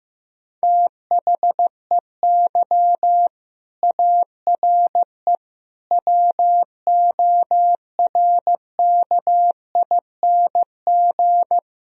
Morse code, heard as a keyed tone, about 15 wpm, 700 Hz. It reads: THEY ARE WORKING
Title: Sentences from Top 1000 Words - Review Part 4 - 15wpm